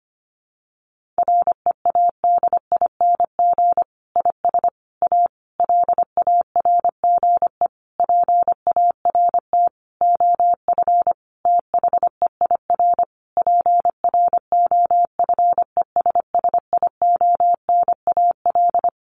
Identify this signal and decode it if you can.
Morse code, heard as a keyed tone, about 25 wpm, 700 Hz. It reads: READING IS A LARGE PART OF THEIR PROFESSIONAL
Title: Sentences from Top 1400 Words 25wpm